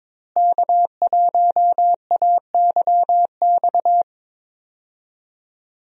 Morse code, heard as a keyed tone, about 22 wpm, 700 Hz. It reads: K1AYX
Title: USA Call Signs - 5 Characters - Repeated 3x - 22wpm